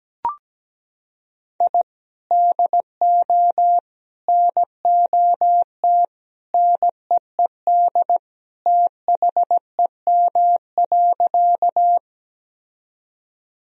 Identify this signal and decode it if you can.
Morse code, heard as a keyed tone, about 17 wpm, 700 Hz. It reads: I DO NOT NEED THEM.
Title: Sentences from Top 1000 Words - Review Part 1 - 17wpm